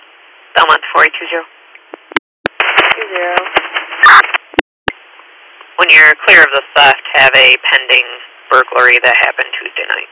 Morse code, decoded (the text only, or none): none